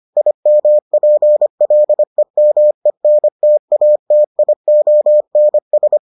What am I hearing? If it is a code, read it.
Morse code, heard as a keyed tone, about 25 wpm, 600 Hz. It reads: IMPLEMENTATIONS